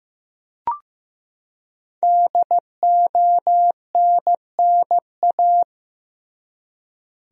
Morse code, heard as a keyed tone, about 15 wpm, 700 Hz. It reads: DONNA
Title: USA Common Names - Max 5 Letters - 15wpm